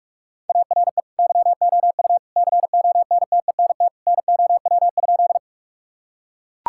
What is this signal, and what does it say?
Morse code, heard as a keyed tone, about 42 wpm, 700 Hz. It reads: ARE YOU CONTENT NOW?
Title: Sentences from Top 1000 Words - Review Part 6 - 45wpm